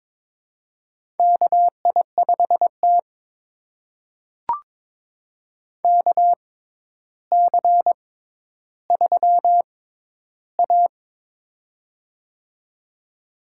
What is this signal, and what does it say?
Morse code, heard as a keyed tone, about 22 wpm, spaced as for 11 wpm, 700 Hz. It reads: KI5T KC3A
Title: USA Call Signs - 4 Characters - 6x Character Spacing - 22wpm